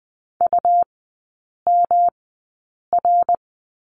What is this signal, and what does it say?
Morse code, heard as a keyed tone, about 20 wpm, spaced as for 8 wpm, 700 Hz. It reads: UMR
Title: 107 Lesson 18e - Letter m - Three Characters 20wpm